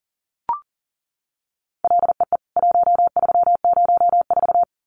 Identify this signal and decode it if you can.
Morse code, heard as a keyed tone, about 38 wpm, 700 Hz. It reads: LEE 1304
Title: CWT Exchange 40wpm